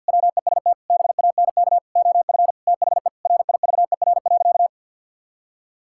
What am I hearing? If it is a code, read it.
Morse code, heard as a keyed tone, about 52 wpm, 700 Hz. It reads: WEST BANK OF THE RIVER.
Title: Sentences from Top 1000 Words - Review Part 6 - 50wpm